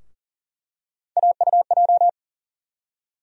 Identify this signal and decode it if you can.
Morse code, heard as a keyed tone, about 38 wpm, 700 Hz. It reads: AUJ
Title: USA Call Sign Postfixes - 3 Letters 40wpm